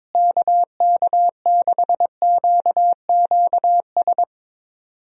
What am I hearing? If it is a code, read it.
Morse code, heard as a keyed tone, about 22 wpm, 700 Hz. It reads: KK6QQS
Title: USA Call Signs - 6 Characters - Repeated 2x - 22wpm